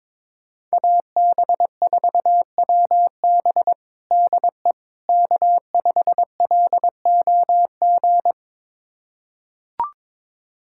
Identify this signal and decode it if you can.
Morse code, heard as a keyed tone, about 22 wpm, 700 Hz. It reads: AB4WB DE K5LOG
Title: USA Callsign DE Callsign - 5 Characters - 22wpm